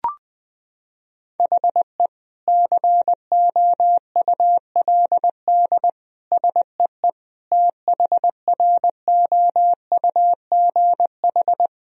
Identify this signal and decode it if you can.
Morse code, heard as a keyed tone, about 21 wpm, 700 Hz. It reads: HE COULD SEE THROUGH
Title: Sentences from Top 1000 Words - Review Part 1 - 20wpm